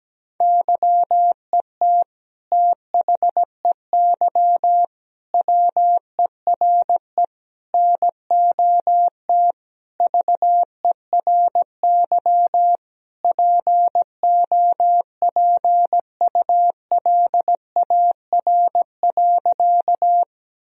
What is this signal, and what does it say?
Morse code, heard as a keyed tone, about 17 wpm, 700 Hz. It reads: YET THEY WERE NOT VERY POPULAR.
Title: Sentences from Top 1000 Words - Review Part 3 - 17wpm